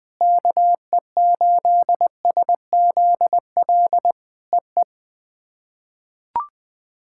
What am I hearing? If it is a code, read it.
Morse code, heard as a keyed tone, about 20 wpm, 700 Hz. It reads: KE8SZL EE
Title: POTA QSO Practice - 20wpm